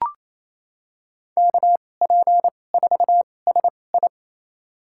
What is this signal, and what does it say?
Morse code, heard as a keyed tone, about 28 wpm, spaced as for 20 wpm, 700 Hz. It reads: KP4SI
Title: USA Call Signs - 5 Characters - 2x Character Spacing - 28wpm